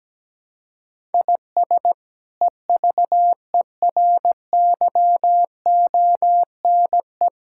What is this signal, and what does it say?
Morse code, heard as a keyed tone, about 17 wpm, 700 Hz. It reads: IS EVERYONE
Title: Sentences from Top 1000 Words - Review Part 5 - 17wpm